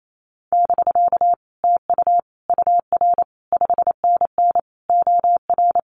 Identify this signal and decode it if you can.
Morse code, heard as a keyed tone, about 28 wpm, 700 Hz. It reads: <BK> TU UR 5NN OR
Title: Common QSO Phrases Encore 28wpm